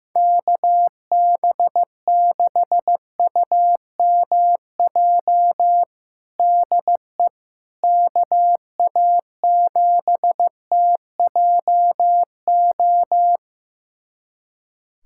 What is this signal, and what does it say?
Morse code, heard as a keyed tone, about 15 wpm, 700 Hz. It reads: KB6UMJ DE KA7TJO
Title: USA Callsign DE Callsign - 6 Characters - 15wpm